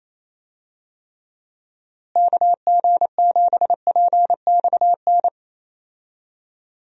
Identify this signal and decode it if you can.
Morse code, heard as a keyed tone, about 27 wpm, 700 Hz. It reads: KG7PXN
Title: USA Call Signs - 6 Characters - Repeated 2x - 28wpm